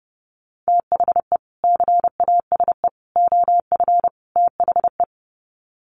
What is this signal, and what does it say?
Morse code, heard as a keyed tone, about 30 wpm, 700 Hz. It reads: THE CASE OF THE
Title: Sets of 4 Words 30wpm